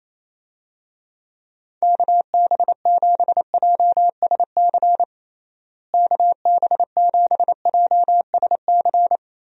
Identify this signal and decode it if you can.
Morse code, heard as a keyed tone, about 28 wpm, 700 Hz. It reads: KB7JSC KB7JSC
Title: USA Call Signs - 6 Characters - Repeated 2x - 28wpm